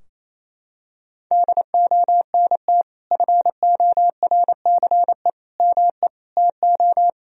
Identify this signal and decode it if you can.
Morse code, heard as a keyed tone, about 28 wpm, 700 Hz. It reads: DONT FORCE ME TO
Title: Sentences from Top 1000 Words - Review Part 3 - 28wpm